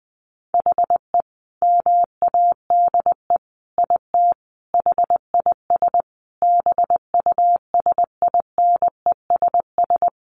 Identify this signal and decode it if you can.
Morse code, heard as a keyed tone, about 20 wpm, 700 Hz. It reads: HE MADE IT HIS BUSINESS
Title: Sentences from Top 1400 Words 20wpm